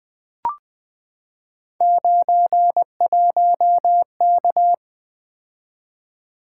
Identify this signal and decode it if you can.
Morse code, heard as a keyed tone, about 20 wpm, 700 Hz. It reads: 91K
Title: Three Letter-Number Combinations - 20wpm